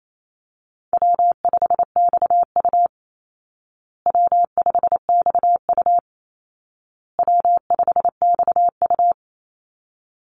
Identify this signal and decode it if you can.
Morse code, heard as a keyed tone, about 28 wpm, 700 Hz. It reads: W5XU W5XU W5XU
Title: Call Signs Repeated 2x - 28wpm